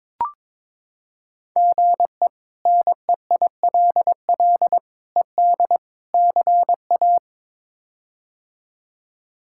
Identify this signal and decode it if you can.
Morse code, heard as a keyed tone, about 21 wpm, 700 Hz. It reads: GE NEILL ED CA
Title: K1USN SST QSO Practice - 22wpm